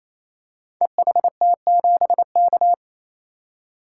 Morse code, heard as a keyed tone, about 28 wpm, 700 Hz. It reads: EHT7K
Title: Code Groups - 5 Letter:Numbers - 28wpm